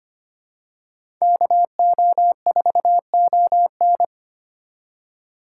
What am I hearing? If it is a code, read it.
Morse code, heard as a keyed tone, about 25 wpm, 700 Hz. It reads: KO4ON
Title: USA Call Signs - 5 Characters - 3x Character Spacing - 25wpm